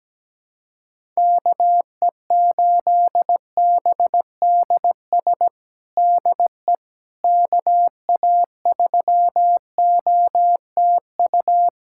Morse code, heard as a keyed tone, about 17 wpm, 700 Hz. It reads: KE8BDS DE KA3OTU
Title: USA Callsign DE Callsign 17wpm